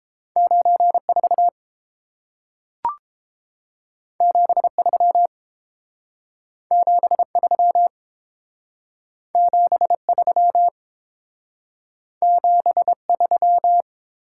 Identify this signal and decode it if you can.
Morse code, heard as a keyed tone, about 30 wpm, 700 Hz. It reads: 94 73 73 73 73